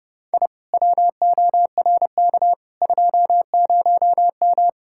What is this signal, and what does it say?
Morse code, heard as a keyed tone, about 40 wpm, 700 Hz. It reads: I WORK 20M